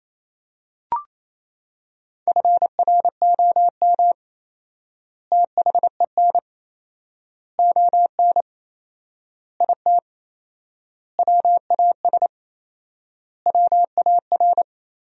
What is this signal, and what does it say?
Morse code, heard as a keyed tone, about 27 wpm, 700 Hz. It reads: FROM THEN ON IT WAS WAR
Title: 4x Word-Spacing - Sentences from Top 400 Words - 28wpm